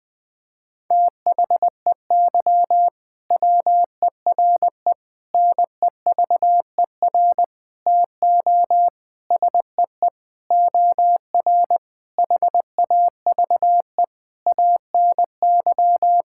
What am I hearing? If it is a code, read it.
Morse code, heard as a keyed tone, about 20 wpm, 700 Hz. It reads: THEY WERE NEVER TO SEE OR HAVE ANY
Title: Sentences from Top 1000 Words - Review Part 6 - 20wpm